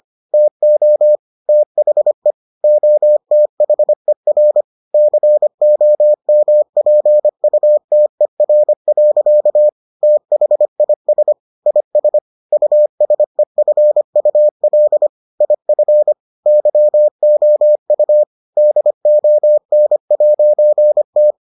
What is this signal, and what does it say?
Morse code, heard as a keyed tone, about 25 wpm, 600 Hz. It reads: TO THE OTHER COMPUTER. THIS IS USEFUL IF YOU DON'T